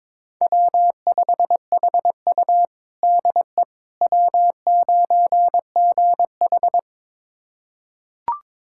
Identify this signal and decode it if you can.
Morse code, heard as a keyed tone, about 22 wpm, 700 Hz. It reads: W5HU DE W9GH
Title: USA Callsign DE Callsign - 4 Characters - 22wpm